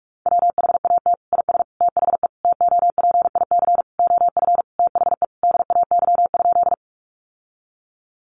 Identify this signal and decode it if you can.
Morse code, heard as a keyed tone, about 46 wpm, 700 Hz. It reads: WHAT IS THE TOPIC OF THE DAY?
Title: Sentences from Top 1200 Words 45wpm